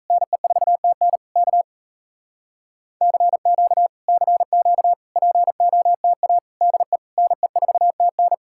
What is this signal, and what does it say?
Morse code, heard as a keyed tone, about 37 wpm, 700 Hz. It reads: NE4TN K CQ CQ POTA DE NE4TN